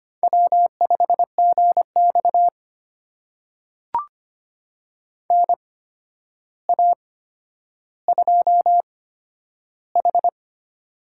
Morse code, heard as a keyed tone, about 25 wpm, spaced as for 11 wpm, 700 Hz. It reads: W5GX NA2H
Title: USA Call Signs - 4 Characters - 8x Character Spacing - 25wpm